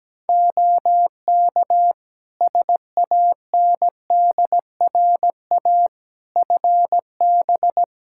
Morse code, heard as a keyed tone, about 17 wpm, 700 Hz. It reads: OK SANDRA FB
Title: Practice QSO - Exchange 2 - Info - 17wpm